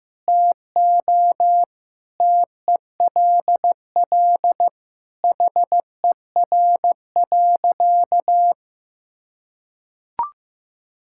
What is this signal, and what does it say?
Morse code, heard as a keyed tone, about 15 wpm, 700 Hz. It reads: TO TELL HER.
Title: Sentences from Top 1000 Words - Review Part 5 - 15wpm